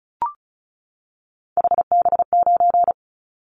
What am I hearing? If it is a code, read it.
Morse code, heard as a keyed tone, about 36 wpm, 700 Hz. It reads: HB9
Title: Callsign Prefixes for Top 25 DXCC - 35wpm